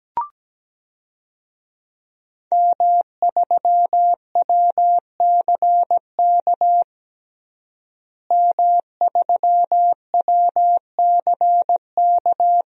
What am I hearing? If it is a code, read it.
Morse code, heard as a keyed tone, about 17 wpm, 700 Hz. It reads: M3WCK M3WCK
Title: DXCC-100 Call Signs - 5 Characters - Repeated 2x - 17wpm